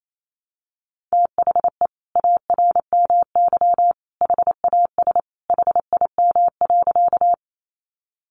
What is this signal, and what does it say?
Morse code, heard as a keyed tone, about 28 wpm, 700 Hz. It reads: THE ARMY HAS HIM.
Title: Sentences from Top 800 Words 28wpm